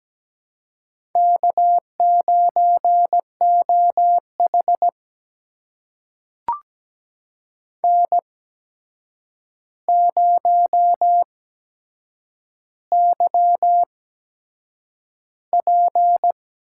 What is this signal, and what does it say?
Morse code, heard as a keyed tone, about 17 wpm, spaced as for 7 wpm, 700 Hz. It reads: K9OH N0YP
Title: USA Call Signs - 4 Characters - 8x Character Spacing - 17wpm